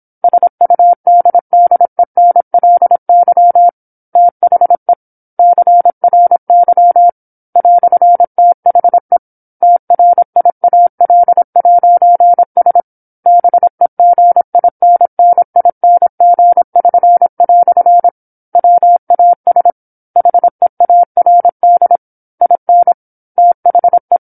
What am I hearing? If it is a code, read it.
Morse code, heard as a keyed tone, about 27 wpm, 700 Hz. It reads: SUDDENLY THE CRY "THE TRIAL'S BEGINNING<SN>" WAS HEARD IN THE
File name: 08-Chapter-08 26 WPM